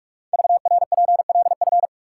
Morse code, heard as a keyed tone, about 44 wpm, 700 Hz. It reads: VRPLF
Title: Code Groups - 5 Letters - 45wpm